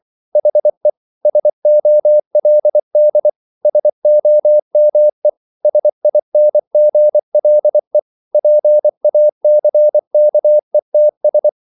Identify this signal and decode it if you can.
Morse code, heard as a keyed tone, about 24 wpm, 600 Hz. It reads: HE SOLD SOME SINGLE PACKETS